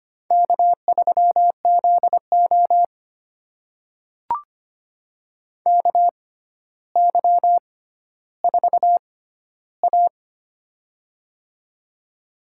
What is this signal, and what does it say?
Morse code, heard as a keyed tone, about 25 wpm, spaced as for 13 wpm, 700 Hz. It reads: K3ZO KY4A
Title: USA Call Signs - 4 Characters - 6x Character Spacing - 25wpm